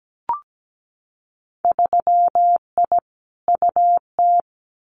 Morse code, heard as a keyed tone, about 17 wpm, 700 Hz. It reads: 3I UT
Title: Winter Field Day Exchanges Encore 17wpm